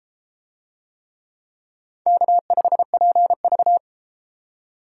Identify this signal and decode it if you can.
Morse code, heard as a keyed tone, about 29 wpm, 700 Hz. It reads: K5PV